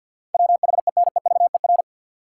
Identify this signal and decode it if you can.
Morse code, heard as a keyed tone, about 50 wpm, 700 Hz. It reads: WHENEVER
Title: Sentences from Top 1200 Words - New Word Drill Alt - 50wpm